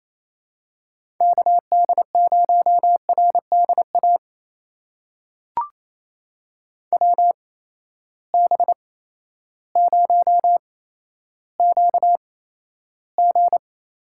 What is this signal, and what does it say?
Morse code, heard as a keyed tone, about 28 wpm, spaced as for 13 wpm, 700 Hz. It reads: KD0RDA WB0QG
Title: USA Call Signs - 6 Characters - 8x Character Spacing - 28wpm